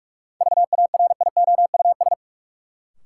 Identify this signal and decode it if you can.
Morse code, heard as a keyed tone, about 46 wpm, 700 Hz. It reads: VARIOUS
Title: Sentences from Top 1000 Words - Review Part 4 - New Word Drill - 45wpm